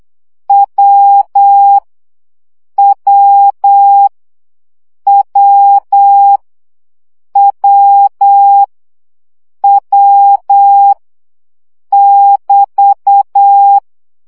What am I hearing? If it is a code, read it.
Morse code, heard as a keyed tone, about 8 wpm, 800 Hz. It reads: W W W W W <BT>
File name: CW gyakorlas w 7 baud